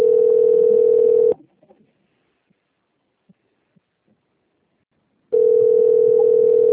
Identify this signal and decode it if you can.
Ringback tone (North America), 440 + 480 Hz.
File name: out- - - - - .4766